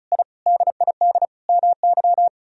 Morse code, heard as a keyed tone, about 34 wpm, 700 Hz. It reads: I DID MY
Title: Sentences from Top 1000 Words - Review Part 4 - 35wpm